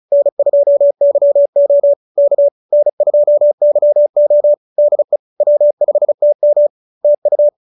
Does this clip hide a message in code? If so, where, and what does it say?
Yes: Morse code from 0.1 to 7.5 s, heard as a keyed tone, about 34 wpm, 590 Hz. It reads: N2YO K N2YO DE W5TM TU